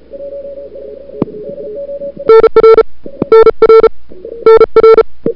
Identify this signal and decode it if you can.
Morse code, heard as a keyed tone, about 34 wpm, 1340 Hz, with noise. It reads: NR NR NR
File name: CWSprint / February 2007 CW Sprint / 01-2007 CW Sprint Hour1